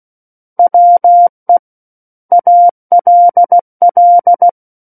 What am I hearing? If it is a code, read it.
Morse code, heard as a keyed tone, about 16 wpm, 700 Hz. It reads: WE ALL
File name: Book One / 07-Chapter 7 - 16 WPM